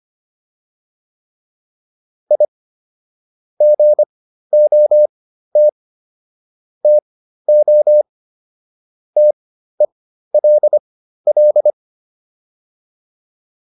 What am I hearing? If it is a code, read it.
Morse code, heard as a keyed tone, about 25 wpm, spaced as for 13 wpm, 600 Hz. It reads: I GOT TO TELL